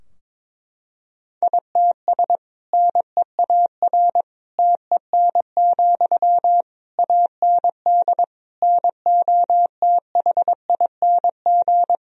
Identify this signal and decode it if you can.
Morse code, heard as a keyed tone, about 22 wpm, 700 Hz. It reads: ITS NEAR TEN, AND NOTHING
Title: Sentences from Top 1000 Words - Review Part 2 - 22wpm